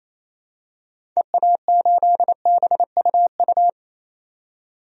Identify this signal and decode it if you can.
Morse code, heard as a keyed tone, about 27 wpm, 700 Hz. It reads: EA8BUU
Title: DXCC-100 Call Signs - 6 Characters - 28wpm